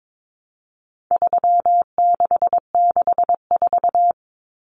Morse code, heard as a keyed tone, about 22 wpm, 700 Hz. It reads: 3664
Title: Common Cut Numbers - 4-Digit 22wpm